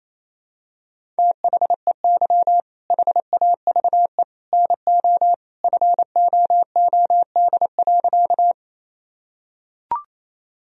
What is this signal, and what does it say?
Morse code, heard as a keyed tone, about 28 wpm, 700 Hz. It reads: THEY HAVE NO FOOD.
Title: Sentences from Top 1000 Words - Review Part 1 - 28wpm